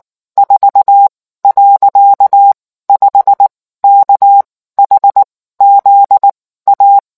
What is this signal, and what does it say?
Morse code, heard as a keyed tone, about 19 wpm, spaced as for 14 wpm, 800 Hz. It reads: 4.5 KHZ A